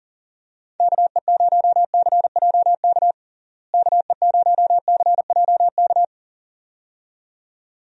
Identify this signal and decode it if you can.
Morse code, heard as a keyed tone, about 40 wpm, 700 Hz. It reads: KE0CJK KE0CJK
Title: USA Call Signs - 6 Characters - Repeated 2x - 40wpm